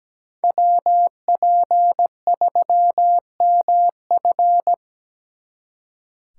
Morse code, heard as a keyed tone, about 20 wpm, 700 Hz. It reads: WP3MF
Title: Speed Racing - USA 2x2 Call Signs 17wpm